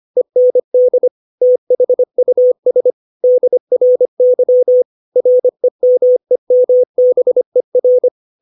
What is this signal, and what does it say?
Morse code, heard as a keyed tone, about 25 wpm, 500 Hz. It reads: END THUS DRY REMEMBER
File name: 632 words / 632 Words 25WPM